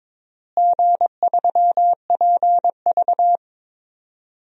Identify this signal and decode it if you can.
Morse code, heard as a keyed tone, about 22 wpm, 700 Hz. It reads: G3PV
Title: DXCC-100 Call Signs - 4 Characters - Repeated 3x - 22wpm